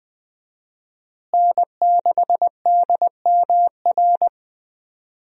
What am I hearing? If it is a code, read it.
Morse code, heard as a keyed tone, about 20 wpm, 700 Hz. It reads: N6DMR
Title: USA Call Signs - 5 Characters - Repeated 3x - 20wpm